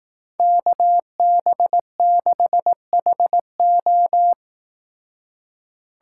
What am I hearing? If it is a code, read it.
Morse code, heard as a keyed tone, about 18 wpm, 700 Hz. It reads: KB6HO
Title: Speed Racing - USA Call Signs 15wpm